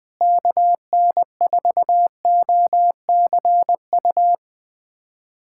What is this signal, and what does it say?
Morse code, heard as a keyed tone, about 21 wpm, 700 Hz. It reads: KN4OCU
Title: USA Callsign DE Callsign - 6 Characters - 20wpm